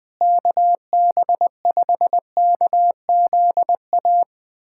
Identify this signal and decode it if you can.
Morse code, heard as a keyed tone, about 21 wpm, 700 Hz. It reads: KB5KZA
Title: USA Callsign DE Callsign - 6 Characters - 20wpm